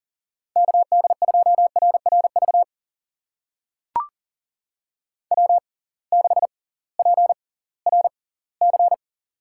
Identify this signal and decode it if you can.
Morse code, heard as a keyed tone, about 40 wpm, spaced as for 22 wpm, 700 Hz. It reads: KD2RRV W6PRC
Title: USA Call Signs - 6x Character Spacing - 40wpm